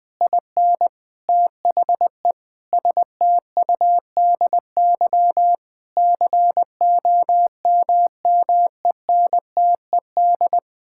Morse code, heard as a keyed tone, about 21 wpm, 700 Hz. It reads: IN THE STUDY COMMENTED
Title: Sentences with 7-Word Phrases - 20wpm